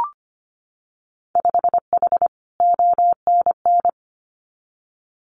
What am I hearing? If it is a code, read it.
Morse code, heard as a keyed tone, about 25 wpm, 700 Hz. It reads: 5H ONN
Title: Winter Field Day Exchanges Encore 25wpm